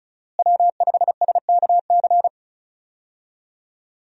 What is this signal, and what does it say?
Morse code, heard as a keyed tone, about 36 wpm, 700 Hz. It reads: W5SKC
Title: USA Call Signs - 5 Characters - Repeated 2x - 35wpm